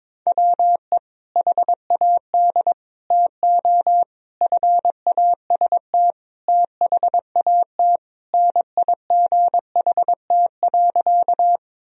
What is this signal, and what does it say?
Morse code, heard as a keyed tone, about 22 wpm, 700 Hz. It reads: WE HAD TO FAST THAT NIGHT.
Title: Sentences from Top 1000 Words - Review Part 2 - 22wpm